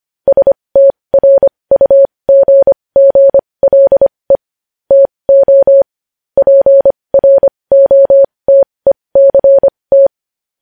Morse code, heard as a keyed tone, about 25 wpm, spaced as for 20 wpm, 550 Hz. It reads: STRUGGLE TO PROTECT
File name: podcast / News headlines-1x-25x20